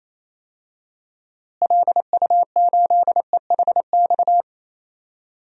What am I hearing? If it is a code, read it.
Morse code, heard as a keyed tone, about 28 wpm, 700 Hz. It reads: LU8EHX
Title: DXCC-100 Call Signs - 6 Characters - Repeated 3x - 28wpm